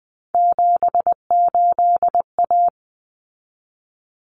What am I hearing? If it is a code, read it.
Morse code, heard as a keyed tone, about 19 wpm, 700 Hz. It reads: 78A
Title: Common Cut Numbers - 3-Digit 20wpm